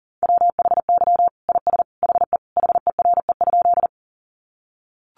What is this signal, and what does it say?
Morse code, heard as a keyed tone, about 40 wpm, 700 Hz. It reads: WHY IS HE HERE?
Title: Sentences from Top 200 Words 40wpm